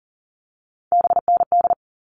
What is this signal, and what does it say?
Morse code, heard as a keyed tone, about 40 wpm, 700 Hz. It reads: BND
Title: All Cut Numbers - 3-Digit 40wpm